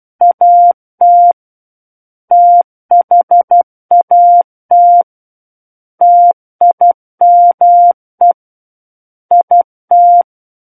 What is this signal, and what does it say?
Morse code, heard as a keyed tone, about 12 wpm, 700 Hz. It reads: AT THAT TIME IT
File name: Book One / 03-Chapter 3 - 12 WPM